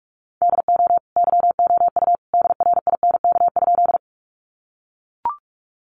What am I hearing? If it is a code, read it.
Morse code, heard as a keyed tone, about 44 wpm, 700 Hz. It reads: DO YOU DRINK?
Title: Sentences from Top 1500 Words 45wpm